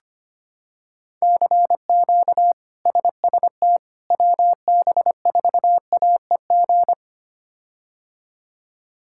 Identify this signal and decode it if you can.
Morse code, heard as a keyed tone, about 25 wpm, 700 Hz. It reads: CQ SST WB4AEG
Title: K1USN SST QSO Practice - 25wpm